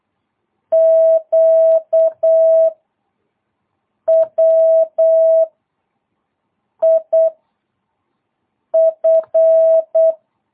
Morse code, heard as a keyed tone, about 8 wpm, 660 Hz. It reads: Q W I F
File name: jh8cbh / 2022cw